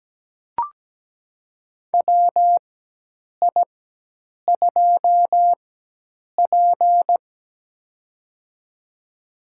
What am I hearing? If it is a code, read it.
Morse code, heard as a keyed tone, about 17 wpm, spaced as for 8 wpm, 700 Hz. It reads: WI2P
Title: USA Call Signs - 4 Characters - 4x Character Spacing - 17wpm